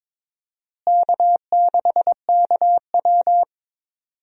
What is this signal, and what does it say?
Morse code, heard as a keyed tone, about 22 wpm, 700 Hz. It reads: K6KW
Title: USA Call Signs - 4 Characters - Repeated 2x - 22wpm